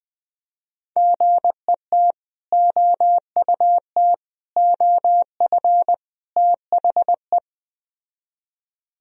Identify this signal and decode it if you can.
Morse code, heard as a keyed tone, about 21 wpm, 700 Hz. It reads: GET OUT OF THE